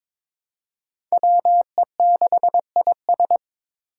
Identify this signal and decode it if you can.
Morse code, heard as a keyed tone, about 22 wpm, 700 Hz. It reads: WE6IS
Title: Five Challenging Letter-Number Combinations 22wpm